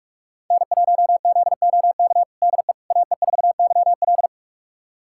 Morse code, heard as a keyed tone, about 43 wpm, 700 Hz. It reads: N1ZOK DE AE4YL
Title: USA Callsign DE Callsign - 5 Characters - 45wpm